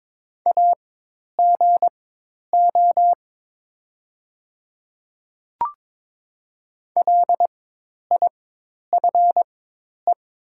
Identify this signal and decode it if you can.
Morse code, heard as a keyed tone, about 22 wpm, spaced as for 10 wpm, 690 Hz. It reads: AGO LIFE
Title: 4x Character Spacing - Top 500 Words - 22wpm